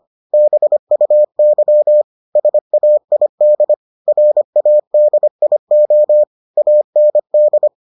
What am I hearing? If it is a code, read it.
Morse code, heard as a keyed tone, about 25 wpm, 600 Hz. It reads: BUY SAID RADIO AND